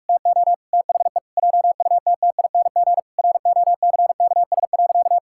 Morse code, heard as a keyed tone, about 48 wpm, 700 Hz. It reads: TO THE JUTTING ROCKS.
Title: Pangram Challenge v2 - 45wpm